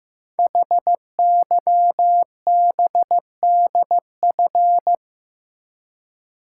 Morse code, heard as a keyed tone, about 15 wpm, 700 Hz. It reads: HYBDF
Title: Code Groups - 5 Letter:Numbers - 15wpm